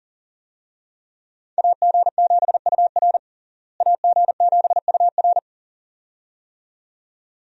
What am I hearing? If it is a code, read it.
Morse code, heard as a keyed tone, about 40 wpm, 700 Hz. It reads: AG7UR AG7UR
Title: USA Call Signs - 5 Characters - Repeated 2x - 40wpm